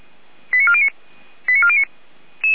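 Five-tone selective call (ZVEI), heard as quick runs of five tones, tones of about 70 ms. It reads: 89409 89409
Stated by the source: ZVEI five-tone selective call received by SDR.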